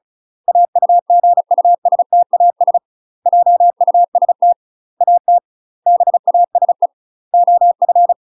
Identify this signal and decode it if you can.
Morse code, heard as a keyed tone, about 34 wpm, 700 Hz. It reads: AUGUSTAS JUST AT BASE OF
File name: UK MASTER QSOs TRAINING / 35WPM / UK MASTER QSOs 0030